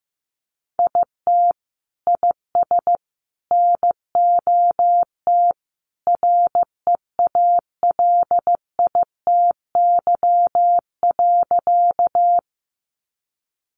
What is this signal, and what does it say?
Morse code, heard as a keyed tone, about 15 wpm, 700 Hz. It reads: IT IS NOT REALITY.
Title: Sentences from Top 1200 Words 15wpm